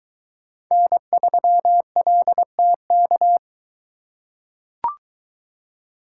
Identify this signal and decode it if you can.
Morse code, heard as a keyed tone, about 21 wpm, 700 Hz. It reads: N3LTK